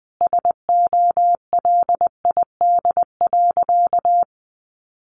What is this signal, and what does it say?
Morse code, heard as a keyed tone, about 20 wpm, 700 Hz. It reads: SOLID.
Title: Sentences from Top 1500 Words 20wpm